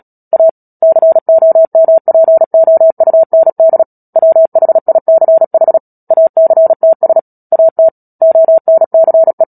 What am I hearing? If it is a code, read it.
Morse code, heard as a keyed tone, about 37 wpm, 650 Hz. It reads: A COMPOUND WHICH ACTS AT ONCE
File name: Book Two / 10-Chapter 10 - Epilogue - 36 WPM